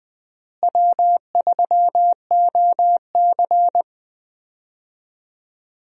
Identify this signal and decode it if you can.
Morse code, heard as a keyed tone, about 20 wpm, 700 Hz. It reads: W3OC
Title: USA Call Signs - 4 Characters - Repeated 2x - 20wpm